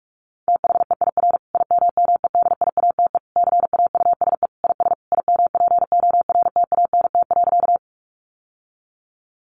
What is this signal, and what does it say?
Morse code, heard as a keyed tone, about 44 wpm, 700 Hz. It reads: THEIR IMMEDIATE CAUSE IS IMPORTANT.
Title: Sentences from Top 1200 Words 45wpm